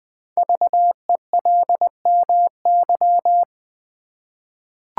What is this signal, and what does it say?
Morse code, heard as a keyed tone, about 20 wpm, 700 Hz. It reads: VELMY
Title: Code Groups - 5 Letters - 20wpm